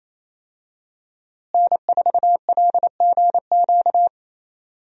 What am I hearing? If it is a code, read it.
Morse code, heard as a keyed tone, about 28 wpm, 700 Hz. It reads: N4LGQ
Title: USA Call Signs - 5 Characters - Repeated 2x - 28wpm